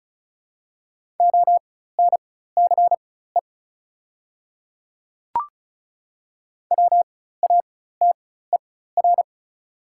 Morse code, heard as a keyed tone, about 36 wpm, spaced as for 16 wpm, 700 Hz. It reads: ONCE WATER
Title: 4x Character Spacing - Top 300 Words - 35wpm